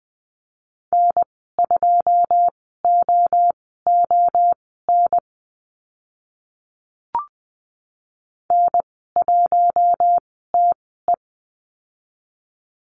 Morse code, heard as a keyed tone, about 20 wpm, spaced as for 15 wpm, 700 Hz. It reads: N2OON N1TE
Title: 082 Lesson 14g - Number 2 - Call Signs 20wpm